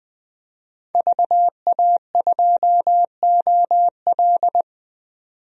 Morse code, heard as a keyed tone, about 20 wpm, 700 Hz. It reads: VA2OL
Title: DXCC-100 Call Signs - 5 Characters - 20wpm